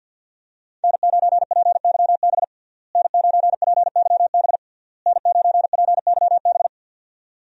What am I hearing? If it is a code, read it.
Morse code, heard as a keyed tone, about 50 wpm, 700 Hz. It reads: N9PYB N9PYB N9PYB
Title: USA Call Signs - 5 Characters - Repeated 3x - 50wpm